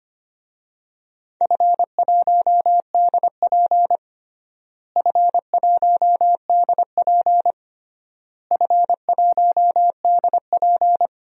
Morse code, heard as a keyed tone, about 25 wpm, 700 Hz. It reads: F1DP F1DP F1DP
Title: DXCC-100 Call Signs - 4 Characters - Repeated 3x - 25wpm